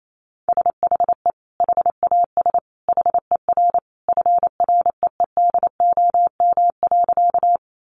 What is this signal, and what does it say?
Morse code, heard as a keyed tone, about 28 wpm, 700 Hz. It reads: SHE HAS HER FREEDOM.
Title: Sentences from Top 800 Words 28wpm